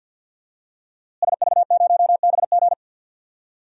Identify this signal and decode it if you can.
Morse code, heard as a keyed tone, about 51 wpm, 700 Hz. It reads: SV0BG
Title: DXCC-100 Call Signs - 5 Characters - Repeated 3x - 50wpm